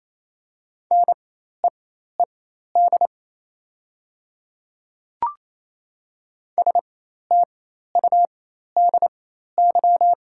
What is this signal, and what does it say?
Morse code, heard as a keyed tone, about 28 wpm, spaced as for 13 wpm, 700 Hz. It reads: NEED STUDY
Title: 4x Character Spacing - Top 500 Words - 28wpm